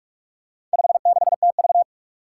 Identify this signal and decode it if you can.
Morse code, heard as a keyed tone, about 44 wpm, 700 Hz. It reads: 56TV
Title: Four Challenging Letter-Number Combinations 45wpm